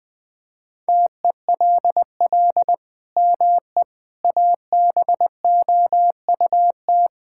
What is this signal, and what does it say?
Morse code, heard as a keyed tone, about 20 wpm, 700 Hz. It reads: TELL ME ABOUT
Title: Sentences from Top 1600 Words 20wpm